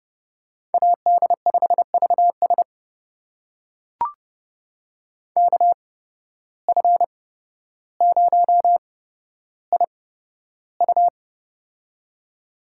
Morse code, heard as a keyed tone, about 30 wpm, spaced as for 13 wpm, 700 Hz. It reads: AD5VS KF0IU
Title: USA Call Signs - 5 Characters - 8x Character Spacing - 30wpm